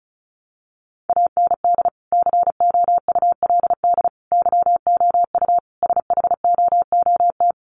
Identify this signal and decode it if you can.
Morse code, heard as a keyed tone, about 34 wpm, 700 Hz. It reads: AND COULD YOU SHOOT
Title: Sentences from Top 1200 Words 35wpm